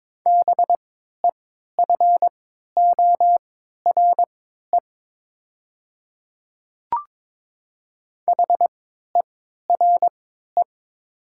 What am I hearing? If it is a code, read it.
Morse code, heard as a keyed tone, about 22 wpm, spaced as for 12 wpm, 700 Hz. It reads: BEFORE HERE
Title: 3x Character Spacing - Top 200 Words - 22wpm